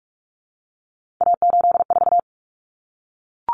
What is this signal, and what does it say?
Morse code, heard as a keyed tone, about 44 wpm, 700 Hz. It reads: A84
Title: Common Cut Numbers - 3-Digit 45wpm